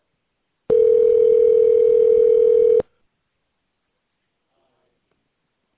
Ringback tone (North America), 440 + 480 Hz.